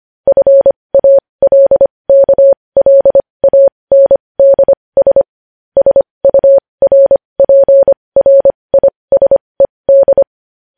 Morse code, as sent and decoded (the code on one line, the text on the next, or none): ..-. .- .-.. -.- .-.. .- -. -.. ... / ... ..- .-. .--. .-. .. ... -..
FALKLANDS SURPRISED